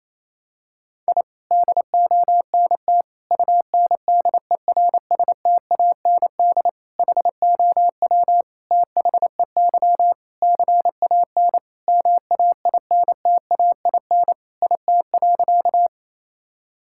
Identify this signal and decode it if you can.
Morse code, heard as a keyed tone, about 28 wpm, 700 Hz. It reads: I DONT UNDERSTAND HOW THEY CAN MAINTAIN IT.
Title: Sentences from Top 1000 Words - Review Part 5 - 28wpm